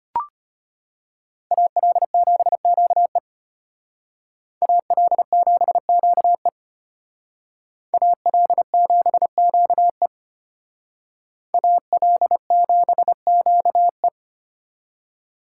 Morse code, heard as a keyed tone, about 32 wpm, 700 Hz. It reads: AL7QE AL7QE AL7QE AL7QE